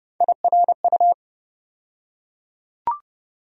Morse code, heard as a keyed tone, about 30 wpm, 700 Hz. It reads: IRU
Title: USA Call Sign Postfixes - 3 Letters 30wpm